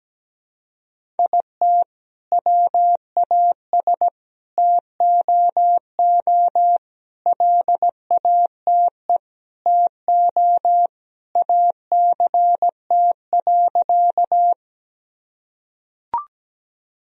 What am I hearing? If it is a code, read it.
Morse code, heard as a keyed tone, about 17 wpm, 700 Hz. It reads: IT WAS TOO LATE TO ACT.
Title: Sentences from Top 1000 Words - Review Part 1 - 17wpm